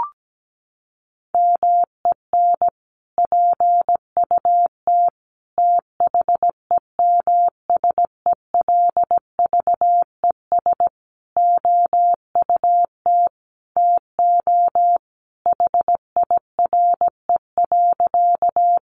Morse code, heard as a keyed tone, about 17 wpm, 700 Hz. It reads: MEN PUT THEMSELVES OUT TO HIRE.
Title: Sentences from Top 1400 Words 17wpm